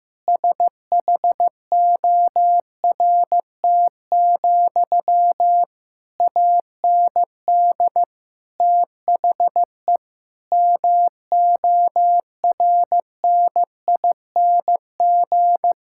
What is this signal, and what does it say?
Morse code, heard as a keyed tone, about 15 wpm, 700 Hz. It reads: SHORT, AND THE MORNING
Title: Sentences from Top 1600 Words 15wpm